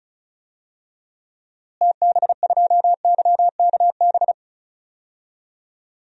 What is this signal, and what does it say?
Morse code, heard as a keyed tone, about 36 wpm, 700 Hz. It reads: TB2YKB
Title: DXCC-100 Call Signs - 6 Characters - Repeated 2x - 35wpm